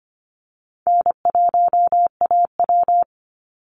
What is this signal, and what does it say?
Morse code, heard as a keyed tone, about 25 wpm, 700 Hz. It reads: N1AW
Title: NA SOTA Chaser Callsigns 25wpm